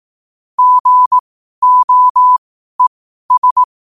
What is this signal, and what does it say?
Morse code, heard as a keyed tone, about 18 wpm, spaced as for 12 wpm, 1000 Hz. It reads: GOES